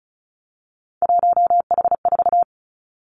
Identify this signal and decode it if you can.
Morse code, heard as a keyed tone, about 36 wpm, 700 Hz. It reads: 1H4
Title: Three Challenging Letter-Number Combinations 35wpm